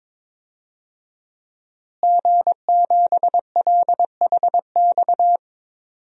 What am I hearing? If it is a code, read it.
Morse code, heard as a keyed tone, about 22 wpm, 700 Hz. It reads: G7LHX
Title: DXCC-100 Call Signs - 5 Characters - Repeated 3x - 22wpm